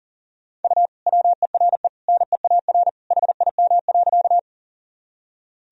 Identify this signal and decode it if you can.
Morse code, heard as a keyed tone, about 40 wpm, 700 Hz. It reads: U WERE NEAR HIM.
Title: Sentences from Top 200 Words and Abbreviations 40wpm